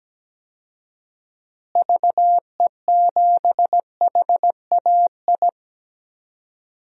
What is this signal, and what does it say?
Morse code, heard as a keyed tone, about 17 wpm, 700 Hz. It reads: VE7HAI
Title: DXCC-100 Call Signs - 6 Characters - Repeated 2x - 17wpm